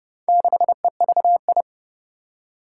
Morse code, heard as a keyed tone, about 30 wpm, 700 Hz. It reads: 6EVI
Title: Four Challenging Letter-Number Combinations 30wpm